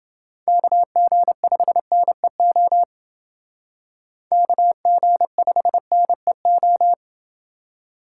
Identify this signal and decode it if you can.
Morse code, heard as a keyed tone, about 29 wpm, 700 Hz. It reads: KG5NEO KG5NEO